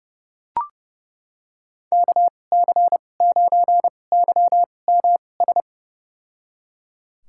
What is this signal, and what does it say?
Morse code, heard as a keyed tone, about 31 wpm, spaced as for 22 wpm, 700 Hz. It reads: KC9YMS
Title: USA Call Signs - 2x Character Spacing - 30wpm